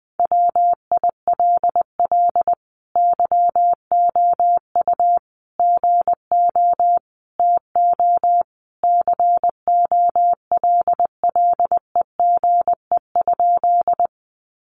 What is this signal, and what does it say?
Morse code, heard as a keyed tone, about 20 wpm, 700 Hz. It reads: WILL YOU GO TO COLLEGE?
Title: Sentences from Top 1300 Words 20wpm